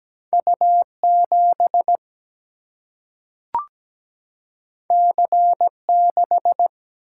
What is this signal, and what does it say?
Morse code, heard as a keyed tone, about 17 wpm, 700 Hz. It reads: U 7 C 6
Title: Single-Letter Single-Number 17wpm